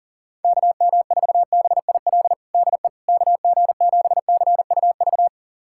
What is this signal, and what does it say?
Morse code, heard as a keyed tone, about 40 wpm, 700 Hz. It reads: KM4BIL DE KG7CUV
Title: USA Callsign DE Callsign - 6 Characters - 40wpm